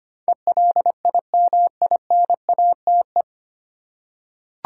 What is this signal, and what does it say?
Morse code, heard as a keyed tone, about 25 wpm, 700 Hz. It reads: ELIMINATE
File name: practice / 25wpm / 9 Letter Words - First 3 Letters Given - 25wpm